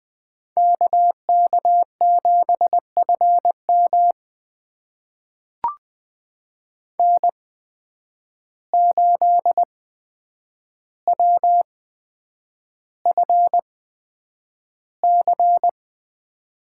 Morse code, heard as a keyed tone, about 20 wpm, spaced as for 9 wpm, 700 Hz. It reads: KK7FM N8WFC
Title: USA Call Signs - 5 Characters - 8x Character Spacing - 20wpm